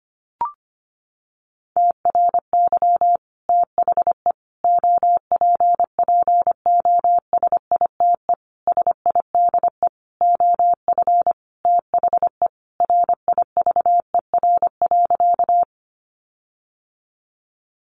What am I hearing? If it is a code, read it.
Morse code, heard as a keyed tone, about 25 wpm, 700 Hz. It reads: TRY THE OPPOSITE SIDE OF THE RIVER.
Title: Sentences from Top 1300 Words 25wpm